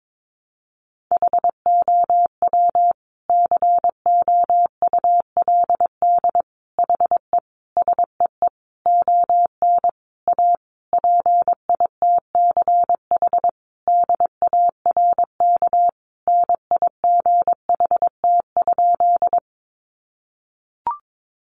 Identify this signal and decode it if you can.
Morse code, heard as a keyed tone, about 22 wpm, 700 Hz. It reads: HOW COULD HE SEE ON A PITCH DARK NIGHT?
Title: Sentences from Top 1300 Words 22wpm